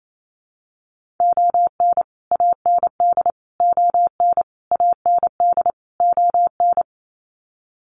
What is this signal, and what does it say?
Morse code, heard as a keyed tone, about 28 wpm, 700 Hz. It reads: ON AND ON AND ON
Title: Sets of 5 Words Encore2 28wpm